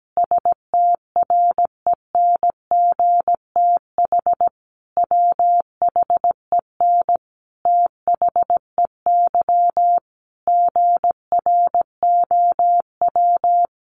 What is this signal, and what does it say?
Morse code, heard as a keyed tone, about 17 wpm, 700 Hz. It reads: STRENGTH WHEN THEY GROW
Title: Sentences from Top 1300 Words 17wpm